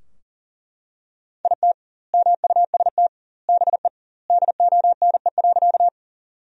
Morse code, heard as a keyed tone, about 40 wpm, 700 Hz. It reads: IT MUST BE DONE.
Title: 2x Word-Spacing - Sentences from Top 500 Words - 40wpm